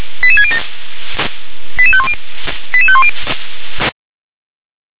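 Five-tone selective call (ZVEI), heard as quick runs of five tones, tones of about 70 ms. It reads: 88607 88510 88510
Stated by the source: ZVEI five-tone call received by SDR.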